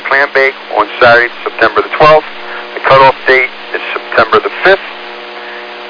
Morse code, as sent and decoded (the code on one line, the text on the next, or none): none